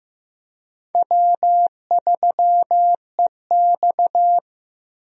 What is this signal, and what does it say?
Morse code, heard as a keyed tone, about 15 wpm, 700 Hz. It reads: W3EX
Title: USA Call Signs - 4 Characters - 15wpm